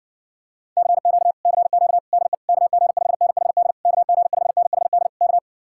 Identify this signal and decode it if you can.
Morse code, heard as a keyed tone, about 60 wpm, 700 Hz. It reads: CQ CQ DE KG5NHD KG5NHD K